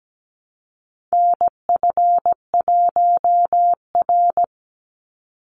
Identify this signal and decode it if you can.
Morse code, heard as a keyed tone, about 17 wpm, 700 Hz. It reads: NF1R
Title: NA SOTA Chaser Callsigns 17wpm